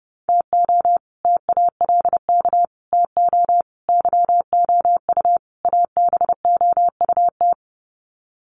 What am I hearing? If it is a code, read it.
Morse code, heard as a keyed tone, about 30 wpm, 700 Hz. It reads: TO TALK TO YOU ABOUT
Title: Sets of 5 Words Encore 30wpm